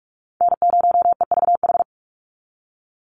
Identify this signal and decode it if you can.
Morse code, heard as a keyed tone, about 46 wpm, 690 Hz. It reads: N0EVH
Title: NA SOTA Chaser Callsigns 45wpm